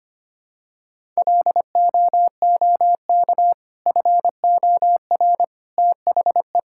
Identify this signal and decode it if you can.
Morse code, heard as a keyed tone, about 25 wpm, 690 Hz. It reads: LOOK FOR THE
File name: practice / 25wpm / Sentences with 2-Word Phrases - 25wpm